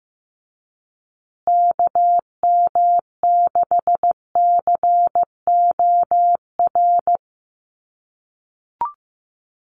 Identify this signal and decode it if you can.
Morse code, heard as a keyed tone, about 15 wpm, 700 Hz. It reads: KM6COR
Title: USA Call Signs Repeated 3x 15wpm